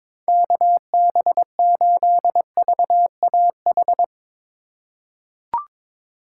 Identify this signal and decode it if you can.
Morse code, heard as a keyed tone, about 21 wpm, 700 Hz. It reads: KB8VAH
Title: USA Call Signs - 6 Characters - 2x Character Spacing - 22wpm